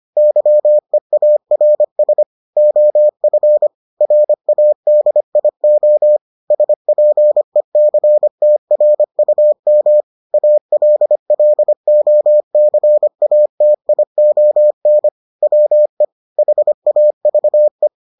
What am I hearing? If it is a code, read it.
Morse code, heard as a keyed tone, about 25 wpm, 600 Hz. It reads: YEARS OF RADIO SPECTRUM ALLOCATION WE HAVE